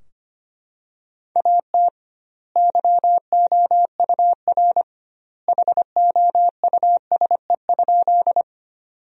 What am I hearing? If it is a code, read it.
Morse code, heard as a keyed tone, about 25 wpm, 700 Hz. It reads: AT YOUR HOUSE?
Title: 2x Word-Spacing - Sentences from Top 200 Words - 25wpm